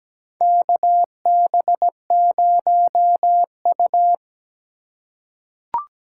Morse code, heard as a keyed tone, about 17 wpm, 700 Hz. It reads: KB0U
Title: USA Call Signs - 2x Character Spacing - 17wpm